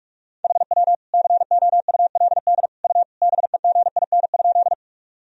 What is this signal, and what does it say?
Morse code, heard as a keyed tone, about 46 wpm, 700 Hz. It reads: HW COULD U BEGIN?
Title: Sentences from Top 300 Words and Abbreviations 45wpm